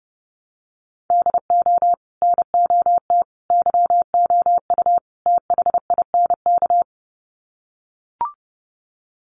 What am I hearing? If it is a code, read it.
Morse code, heard as a keyed tone, about 30 wpm, 700 Hz. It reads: DO NOT YOU THINK
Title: Sets of 4 Words 30wpm